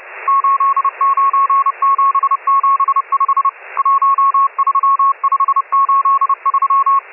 Morse code, heard as a keyed tone, about 30 wpm, 1100 Hz, with radio noise. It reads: 89775 13583
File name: M12- -1010utc